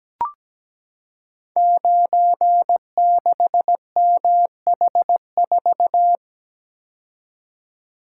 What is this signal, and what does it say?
Morse code, heard as a keyed tone, about 17 wpm, 700 Hz. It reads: 96MH4
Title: Code Groups - 5 Letter:Numbers - 17wpm